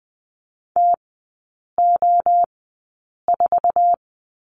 Morse code, heard as a keyed tone, about 19 wpm, spaced as for 8 wpm, 700 Hz. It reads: TO4
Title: 166 Lesson 27e - Number 7 - Three Characters 20wpm